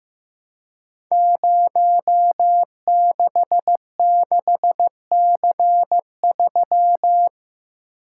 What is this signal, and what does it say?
Morse code, heard as a keyed tone, about 15 wpm, 700 Hz. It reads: 066C3
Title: Code Groups - 5 Letter:Numbers - 15wpm